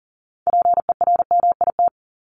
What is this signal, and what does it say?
Morse code, heard as a keyed tone, about 40 wpm, 700 Hz. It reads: PERMIT
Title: Top 1751-2000 Words 40wpm